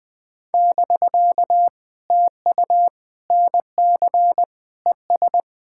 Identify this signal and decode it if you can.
Morse code, heard as a keyed tone, about 20 wpm, 700 Hz. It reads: <BK> TU NC ES
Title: POTA QSO Practice - 20wpm